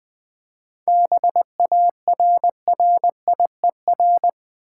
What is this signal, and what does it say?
Morse code, heard as a keyed tone, about 20 wpm, 700 Hz. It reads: BARRIER
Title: 7 Letter Words - First 3 Letters Given - 20wpm